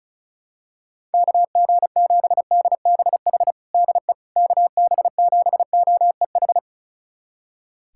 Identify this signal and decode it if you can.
Morse code, heard as a keyed tone, about 34 wpm, 700 Hz. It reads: KG7DBH DE KB7OEH
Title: USA Callsign DE Callsign - 6 Characters - 35wpm